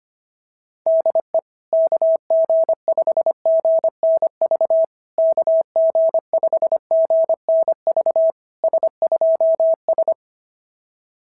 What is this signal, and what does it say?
Morse code, heard as a keyed tone, about 22 wpm, 650 Hz. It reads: DE KG5GNV KG5GNV S2S
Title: SOTA QSO Practice 25wpm